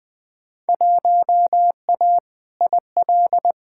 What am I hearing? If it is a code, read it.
Morse code, heard as a keyed tone, about 20 wpm, 700 Hz. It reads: 1A IL
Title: Field Day Exchanges Encore - 20wpm